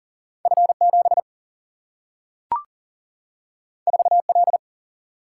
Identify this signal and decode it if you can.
Morse code, heard as a keyed tone, about 40 wpm, 700 Hz. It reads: F 7 4 L
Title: Single-Letter Single-Number 40wpm